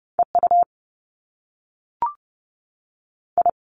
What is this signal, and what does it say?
Morse code, heard as a keyed tone, about 30 wpm, 700 Hz. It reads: E U I
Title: Top Two Letter Combinations in Words 30wpm